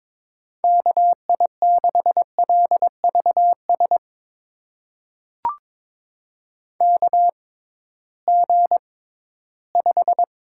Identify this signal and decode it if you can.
Morse code, heard as a keyed tone, about 21 wpm, spaced as for 14 wpm, 700 Hz. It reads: KI6LVS KG5
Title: USA Call Signs - 6 Characters - 6x Character Spacing - 22wpm